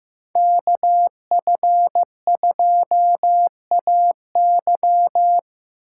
Morse code, heard as a keyed tone, about 15 wpm, 700 Hz. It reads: KF2AY
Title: USA Callsign DE Callsign - 5 Characters - 15wpm